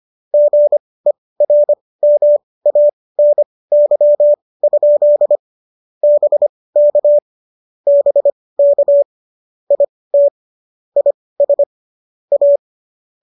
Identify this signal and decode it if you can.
Morse code, heard as a keyed tone, about 25 wpm, spaced as for 18 wpm, 590 Hz. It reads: GERMANY? BK BK IT IS A
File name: qso218 18